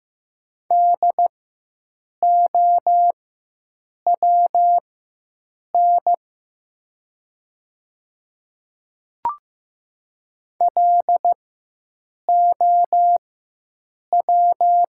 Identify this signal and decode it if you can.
Morse code, heard as a keyed tone, about 15 wpm, spaced as for 7 wpm, 700 Hz. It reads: DOWN LOW
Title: 4x Character Spacing - Top 200 Words - 15wpm